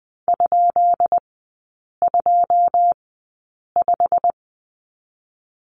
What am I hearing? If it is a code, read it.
Morse code, heard as a keyed tone, about 20 wpm, spaced as for 8 wpm, 690 Hz. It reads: ?25
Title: 159 Lesson 26e - Letter g - Three Characters 20wpm